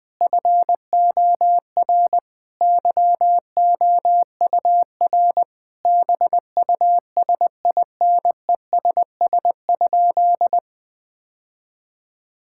Morse code, heard as a keyed tone, about 21 wpm, 700 Hz. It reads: FOR YOUR BUSINESS?
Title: Sentences from Top 1000 Words - Review Part 6 - 20wpm